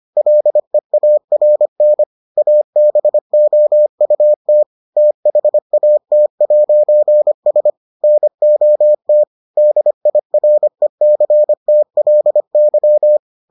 Morse code, heard as a keyed tone, about 25 wpm, 600 Hz. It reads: LEARN ABOUT THAT'S NOT DIRECTLY